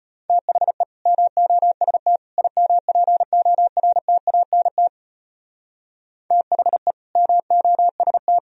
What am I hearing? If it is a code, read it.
Morse code, heard as a keyed tone, about 31 wpm, 700 Hz. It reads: THE MOST IMPORTANT THE MOST
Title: Speed-Racing - Sets of 3 Words 25wpm